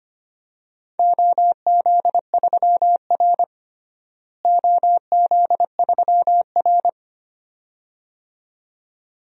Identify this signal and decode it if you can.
Morse code, heard as a keyed tone, about 25 wpm, 700 Hz. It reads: OZ3R OZ3R
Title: DXCC-100 Call Signs - 4 Characters - Repeated 2x - 25wpm